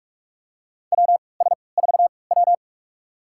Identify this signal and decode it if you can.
Morse code, heard as a keyed tone, about 46 wpm, spaced as for 26 wpm, 700 Hz. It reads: WS4W
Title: USA Call Signs - 3x Character Spacing - 45wpm